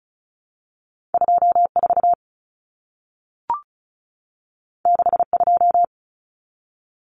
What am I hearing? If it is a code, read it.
Morse code, heard as a keyed tone, about 36 wpm, 700 Hz. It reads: 24 62
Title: Common Cut Numbers - 2-Digit 35wpm